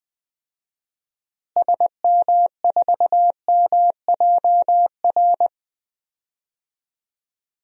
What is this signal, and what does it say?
Morse code, heard as a keyed tone, about 20 wpm, 700 Hz. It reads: SM4MJR
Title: DXCC-100 Call Signs - 6 Characters - Repeated 2x - 20wpm